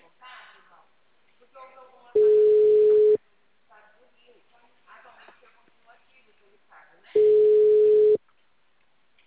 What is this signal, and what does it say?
Ringback tone (Europe, 425 Hz), with speech.